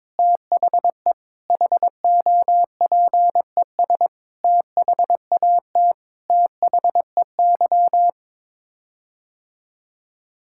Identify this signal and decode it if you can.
Morse code, heard as a keyed tone, about 22 wpm, 700 Hz. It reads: THE HOPES THAT THEY
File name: Sets of 9 Words - 22wpm